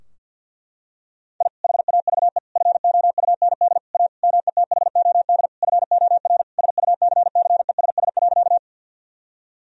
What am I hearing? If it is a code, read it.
Morse code, heard as a keyed tone, about 49 wpm, 700 Hz. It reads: I HAVE FOUND A METHOD FOR SUCCESS.
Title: Sentences from Top 1000 Words - Review Part 3 - 50wpm